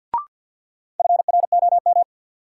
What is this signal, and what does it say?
Morse code, heard as a keyed tone, about 54 wpm, 700 Hz. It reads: FROM
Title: Sentences from Top 100 Words - New Word Drill - 50wpm